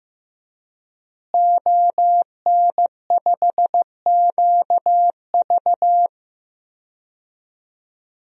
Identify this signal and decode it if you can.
Morse code, heard as a keyed tone, about 15 wpm, 700 Hz. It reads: ON5QV
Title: DXCC-100 Call Signs - 5 Characters - Repeated 2x - 15wpm